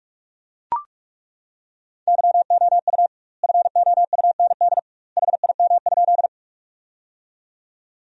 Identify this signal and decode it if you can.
Morse code, heard as a keyed tone, about 42 wpm, 700 Hz. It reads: YOU FOUND HIM?
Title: 2x Word-Spacing - Sentences from Top 300 Words - 45wpm